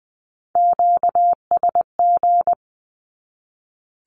Morse code, heard as a keyed tone, about 20 wpm, 700 Hz. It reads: QSG
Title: Q-Codes 20wpm